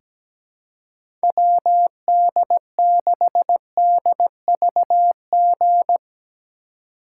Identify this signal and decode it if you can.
Morse code, heard as a keyed tone, about 17 wpm, 700 Hz. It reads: WD6DVG
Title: USA Call Signs - 3x Character Spacing - 17wpm